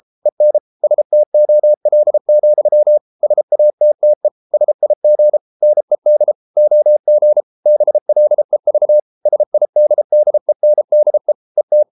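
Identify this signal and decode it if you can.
Morse code, heard as a keyed tone, about 33 wpm, 600 Hz. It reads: EN STOL, SATTE SIG NED OG BLEV SIDDENDE ET
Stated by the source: Morse code at 33 wpm, QRQ.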